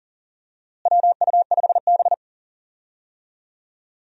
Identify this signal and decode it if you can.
Morse code, heard as a keyed tone, about 39 wpm, 700 Hz. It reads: WU5B
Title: USA Call Signs - 4 Characters - Repeated 2x - 40wpm